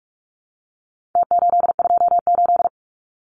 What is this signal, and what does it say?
Morse code, heard as a keyed tone, about 44 wpm, 700 Hz. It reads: T828
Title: Numbers - Contest Style - 4-Digit 45wpm